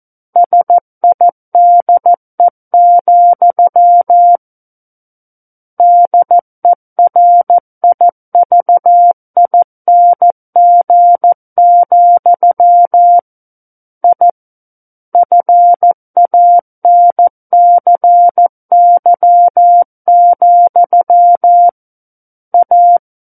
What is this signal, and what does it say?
Morse code, heard as a keyed tone, about 14 wpm, 700 Hz. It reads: SIDE, DERIVING, I FANCY, A
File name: Book One / 05-Chapter 5 - 14 WPM